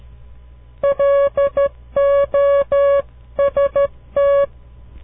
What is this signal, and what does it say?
Morse code, heard as a keyed tone, about 13 wpm, 560 Hz. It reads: LOST
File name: CW Learning Programs / farnsworth / unit 2